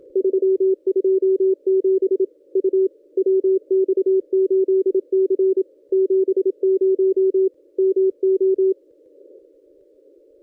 Morse code, heard as a keyed tone, about 27 wpm, 380 Hz, with radio noise. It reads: 327 U WX8C 70 MO
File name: contests / ss-cw-k5tr-op-n5rz -1000